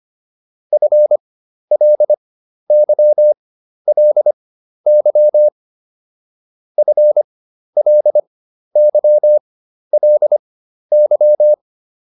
Morse code, heard as a keyed tone, about 25 wpm, spaced as for 12 wpm, 600 Hz. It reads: FLYLY FLYLY